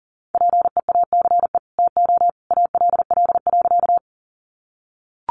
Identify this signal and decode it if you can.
Morse code, heard as a keyed tone, about 40 wpm, 700 Hz. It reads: PEACE TO ALL.
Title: Sentences from Top 1500 Words 40wpm